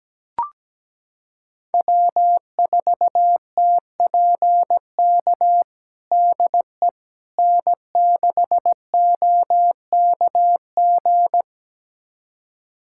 Morse code, heard as a keyed tone, about 17 wpm, 700 Hz. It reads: W4TPK DE N6OKG
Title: USA Callsign DE Callsign - 5 Characters - 17wpm